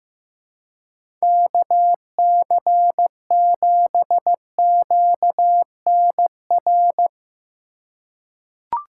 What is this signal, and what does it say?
Morse code, heard as a keyed tone, about 15 wpm, 700 Hz. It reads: KC7QNR
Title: USA Call Signs - 6 Characters - 3x Character Spacing - 15wpm